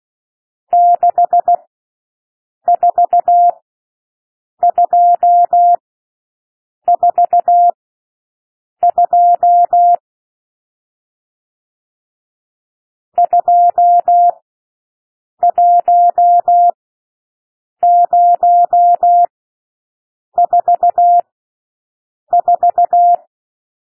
Morse code, heard as a keyed tone, about 16 wpm, spaced as for 6 wpm, 700 Hz. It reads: 64242 21044
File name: SOUND NUMBERS 16 wpm